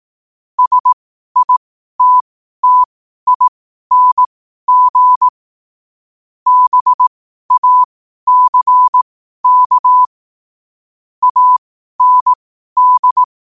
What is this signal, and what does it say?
Morse code, heard as a keyed tone, about 18 wpm, spaced as for 12 wpm, 1000 Hz. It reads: SITTING BACK AND
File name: qotdfiles / 12wpm